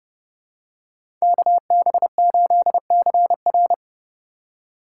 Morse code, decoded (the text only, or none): KB8CR